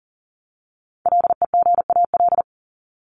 Morse code, heard as a keyed tone, about 40 wpm, 700 Hz. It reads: LEGAL
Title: Top 1001-1250 Words 40wpm